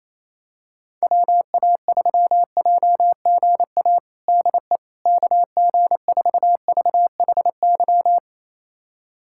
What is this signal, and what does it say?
Morse code, heard as a keyed tone, about 27 wpm, 700 Hz. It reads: WA3JGA DE KG4VHY
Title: USA Callsign DE Callsign 28wpm